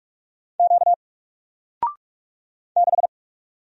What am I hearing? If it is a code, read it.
Morse code, heard as a keyed tone, about 44 wpm, 700 Hz. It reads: Q 6
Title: Single Letter-Number 45wpm